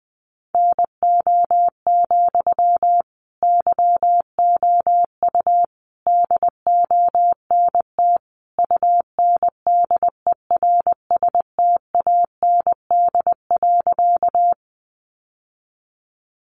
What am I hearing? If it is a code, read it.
Morse code, heard as a keyed tone, about 20 wpm, 700 Hz. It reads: NO, YOU DONT UNDERSTAND.
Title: Sentences from Top 600 Words 20wpm